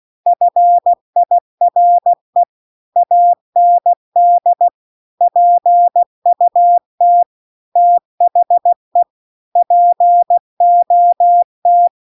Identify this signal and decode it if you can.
Morse code, heard as a keyed tone, about 16 wpm, 700 Hz. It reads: FIRE AND PUT THE POT